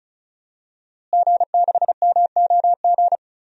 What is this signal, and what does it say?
Morse code, heard as a keyed tone, about 36 wpm, 700 Hz. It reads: G6MOG
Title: DXCC-100 Call Signs - 5 Characters - Repeated 3x - 35wpm